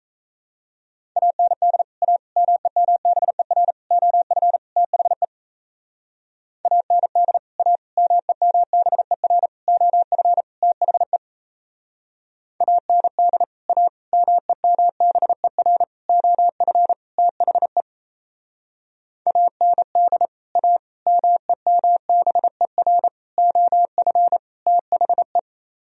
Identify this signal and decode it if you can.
Morse code, heard as a keyed tone, about 36 wpm, 700 Hz. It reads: AND A MEMBER OF THE AND A MEMBER OF THE AND A MEMBER OF THE AND A MEMBER OF THE